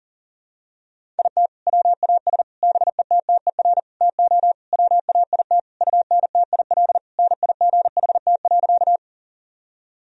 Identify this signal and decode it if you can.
Morse code, heard as a keyed tone, about 40 wpm, 700 Hz. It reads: IT WAS BETTER TO WAIT UNTIL NIGHT.
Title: Sentences from Top 1000 Words - Review Part 1 - 40wpm